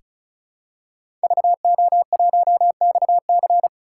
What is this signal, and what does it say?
Morse code, heard as a keyed tone, about 34 wpm, 700 Hz. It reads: VO1XC
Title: K1USN SST QSO Practice - 35wpm